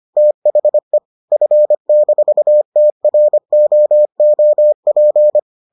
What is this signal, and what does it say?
Morse code, heard as a keyed tone, about 25 wpm, 600 Hz. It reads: THE F-TROOP